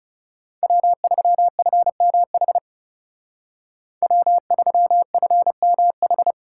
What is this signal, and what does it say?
Morse code, heard as a keyed tone, about 39 wpm, 700 Hz. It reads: W3FMH W3FMH